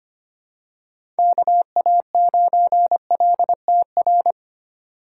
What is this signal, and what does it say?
Morse code, heard as a keyed tone, about 25 wpm, 700 Hz. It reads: KA9LTR
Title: USA Call Signs - 2x Character Spacing - 25wpm